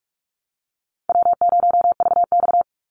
Morse code, heard as a keyed tone, about 44 wpm, 700 Hz. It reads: W0VX
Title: Call Signs Repeated 2x - 45wpm